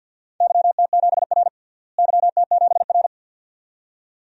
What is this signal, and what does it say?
Morse code, heard as a keyed tone, about 51 wpm, 700 Hz. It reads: YT7R YT7R
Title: DXCC-100 Call Signs - 4 Characters - Repeated 2x - 50wpm